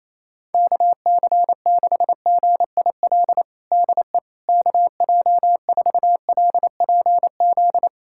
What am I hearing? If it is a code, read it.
Morse code, heard as a keyed tone, about 28 wpm, 700 Hz. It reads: KC6GIL DE KJ4LPZ
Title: USA Callsign DE Callsign - 6 Characters - 28wpm